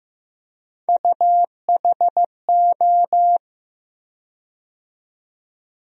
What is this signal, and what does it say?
Morse code, heard as a keyed tone, about 15 wpm, 700 Hz. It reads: UHO
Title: USA Call Sign Postfixes - 3 Letters 15wpm